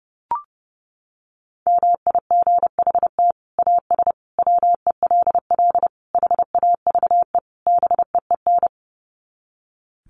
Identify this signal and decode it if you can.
Morse code, heard as a keyed tone, about 30 wpm, 700 Hz. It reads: MIGHT AS WELL HAVE BEEN
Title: Sets of 5 Words Encore2 30wpm